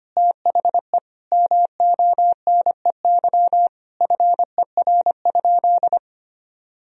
Morse code, heard as a keyed tone, about 25 wpm, 700 Hz. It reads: THE MONEY FER?
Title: Sentences from Top 400 Words and Abbreviations 25wpm